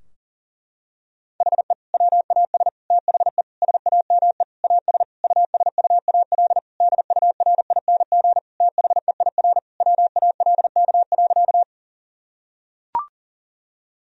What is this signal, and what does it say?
Morse code, heard as a keyed tone, about 40 wpm, 700 Hz. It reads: HE WAS THE SAME AS USUAL DURING THEIR WALK.
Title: Sentences from Top 1000 Words - Review Part 2 - 40wpm